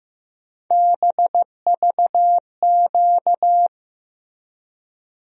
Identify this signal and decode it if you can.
Morse code, heard as a keyed tone, about 15 wpm, 700 Hz. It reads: BVQ
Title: USA Call Sign Postfixes - 3 Letters 15wpm